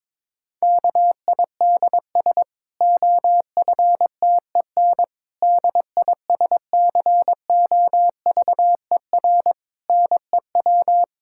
Morse code, heard as a keyed tone, about 22 wpm, 700 Hz. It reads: KIDS OFTEN DISCOVER NEW
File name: Difficult Sentences with Hints - 22wpm